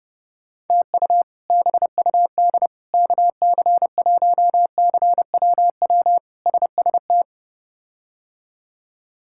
Morse code, heard as a keyed tone, about 30 wpm, 700 Hz. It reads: TU BUD KC1CWW SST
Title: K1USN SST QSO Practice - 30wpm